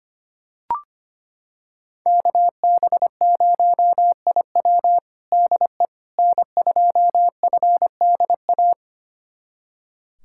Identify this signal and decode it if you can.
Morse code, heard as a keyed tone, about 25 wpm, 700 Hz. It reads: KB0IW DE N2FDA
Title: USA Callsign DE Callsign 25wpm